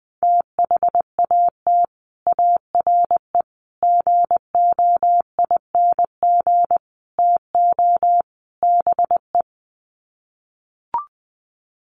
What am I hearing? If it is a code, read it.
Morse code, heard as a keyed tone, about 20 wpm, 700 Hz. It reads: THAT ARE GOING TO BE
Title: Sets of 5 Words Encore2 20wpm